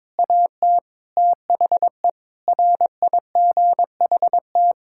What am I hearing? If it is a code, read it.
Morse code, heard as a keyed tone, about 22 wpm, 700 Hz. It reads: AT THE RIGHT
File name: practice / 22wpm / Sets of 8 Words - 22wpm